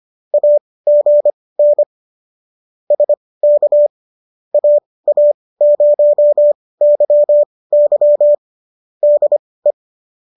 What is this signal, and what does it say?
Morse code, heard as a keyed tone, about 25 wpm, spaced as for 18 wpm, 590 Hz. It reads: AGN SK AA0YY DE